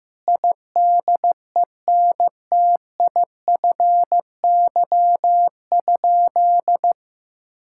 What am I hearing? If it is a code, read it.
Morse code, heard as a keyed tone, about 15 wpm, 700 Hz. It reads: IDENTIFY?
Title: Sentences from Top 1000 Words - Review Part 6 - 15wpm